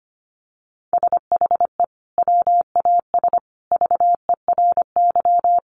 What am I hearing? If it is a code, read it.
Morse code, heard as a keyed tone, about 25 wpm, 700 Hz. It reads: SHE WAS VERY
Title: Sentences from Top 1400 Words 25wpm